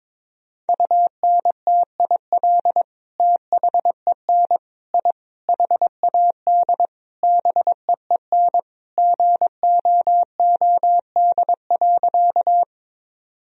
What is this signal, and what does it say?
Morse code, heard as a keyed tone, about 22 wpm, 700 Hz. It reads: UNTIL THEN I HAD BEEN GOOD.
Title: Sentences from Top 1000 Words - Review Part 1 - 22wpm